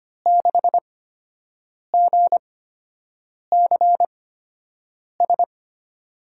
Morse code, heard as a keyed tone, about 25 wpm, spaced as for 7 wpm, 700 Hz. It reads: 6GCS
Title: USA Call Signs - 6 Characters - 8x Character Spacing - 25wpm